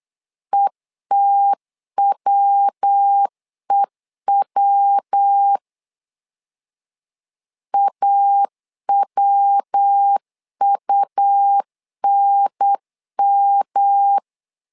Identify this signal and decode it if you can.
Morse code, heard as a keyed tone, about 8 wpm, 800 Hz. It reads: ETWEW AWUNM
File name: LEZIONI / 3 MWRO / 3 MW 8wpm 2501